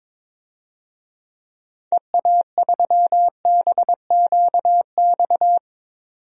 Morse code, heard as a keyed tone, about 22 wpm, 700 Hz. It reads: EA3BQX
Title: DXCC-100 Call Signs - 6 Characters - 22wpm